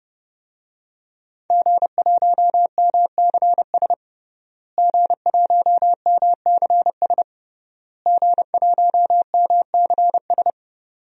Morse code, heard as a keyed tone, about 32 wpm, 700 Hz. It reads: G1MCS G1MCS G1MCS
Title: DXCC-100 Call Signs - 5 Characters - Repeated 3x - 30wpm